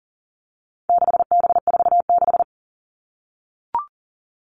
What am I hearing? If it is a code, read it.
Morse code, heard as a keyed tone, about 40 wpm, 700 Hz. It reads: 6B46
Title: All Cut Numbers - 4-Digit 40wpm